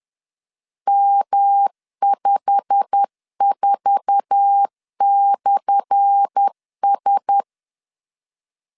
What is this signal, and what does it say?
Morse code, heard as a keyed tone, about 10 wpm, 800 Hz. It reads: M54/S
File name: LEZIONI / 10 SIGNS / 10A SLASH 10wpm 2808